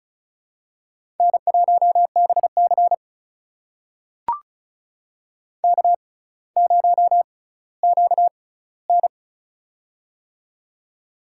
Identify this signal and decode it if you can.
Morse code, heard as a keyed tone, about 36 wpm, spaced as for 18 wpm, 700 Hz. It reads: N1BC K0QN
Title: USA Call Signs - 4 Characters - 6x Character Spacing - 35wpm